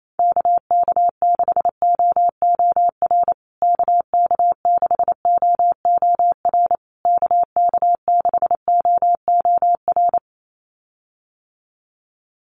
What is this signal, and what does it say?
Morse code, heard as a keyed tone, about 28 wpm, 700 Hz. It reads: KK6OOR KK6OOR KK6OOR
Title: USA Call Signs Repeated 3x 28wpm